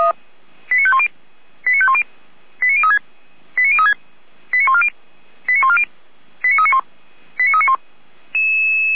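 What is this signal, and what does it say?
Five-tone selective call (ZVEI), heard as quick runs of five tones, tones of about 70 ms. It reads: 89510 89510 89036 89036 89140 89140 89391 89391
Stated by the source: ZVEI five-tone selective call received on SDR, archived by Feuerwehr Orlamünde.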